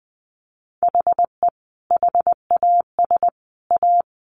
Morse code, heard as a keyed tone, about 20 wpm, 700 Hz. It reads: HE HAS A
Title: Sentences from Top 800 Words 20wpm